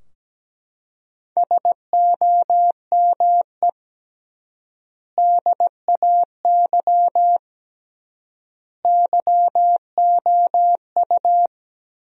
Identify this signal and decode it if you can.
Morse code, heard as a keyed tone, about 17 wpm, 700 Hz. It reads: SOME DAY YOU
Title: 3x Word-Spacing - Sentences from Top 200 Words - 17wpm